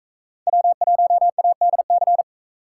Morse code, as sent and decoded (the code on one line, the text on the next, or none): .-- .---- .- -.. -.-.
W1ADC